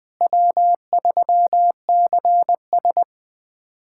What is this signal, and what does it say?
Morse code, heard as a keyed tone, about 20 wpm, 700 Hz. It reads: W3CS
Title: USA Callsign DE Callsign - 4 Characters - 20wpm